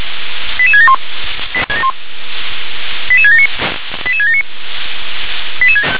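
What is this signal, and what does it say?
Five-tone selective call (ZVEI), heard as quick runs of five tones, tones of about 70 ms. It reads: 88671 88679 88679
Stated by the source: ZVEI five-tone call 88774 received via SDR.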